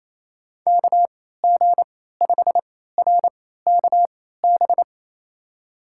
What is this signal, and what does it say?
Morse code, heard as a keyed tone, about 28 wpm, spaced as for 16 wpm, 700 Hz. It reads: KG5RKB
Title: USA Call Signs - 6 Characters - 3x Character Spacing - 28wpm